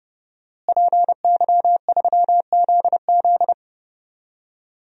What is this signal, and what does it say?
Morse code, heard as a keyed tone, about 30 wpm, 700 Hz. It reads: PY3ZZ
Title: DXCC-100 Call Signs - 5 Characters - Repeated 2x - 30wpm